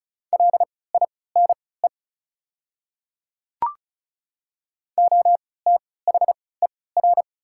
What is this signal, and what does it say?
Morse code, heard as a keyed tone, about 34 wpm, spaced as for 20 wpm, 700 Hz. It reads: LINE OTHER
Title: 3x Character Spacing - Top 300 Words - 35wpm